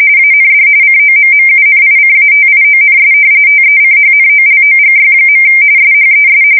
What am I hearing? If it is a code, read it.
RTTY, about 45.45 baud. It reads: AND COMMUNICATIONS TECHNOLOGY. SHES
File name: fsk170 / battleshipnj